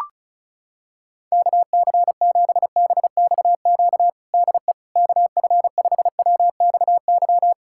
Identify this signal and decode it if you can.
Morse code, heard as a keyed tone, about 36 wpm, 700 Hz. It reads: KC7BXQ DE KF5WXY
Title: USA Callsign DE Callsign 35wpm